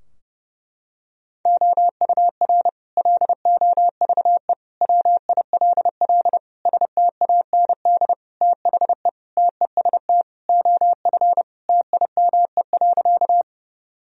Morse code, heard as a keyed tone, about 32 wpm, 700 Hz. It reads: OUR LOVE WILL STAND THE TEST OF TIME.
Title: Sentences from Top 1000 Words - Review Part 3 - 30wpm